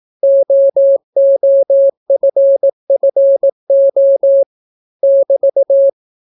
Morse code, heard as a keyed tone, about 18 wpm, 550 Hz. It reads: OOFFO <BT>